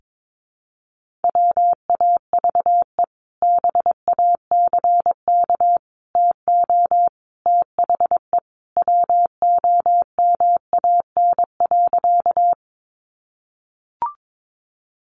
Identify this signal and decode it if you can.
Morse code, heard as a keyed tone, about 22 wpm, 700 Hz. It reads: WAVE BACK TO THE WOMAN.
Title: Sentences from Top 1400 Words 22wpm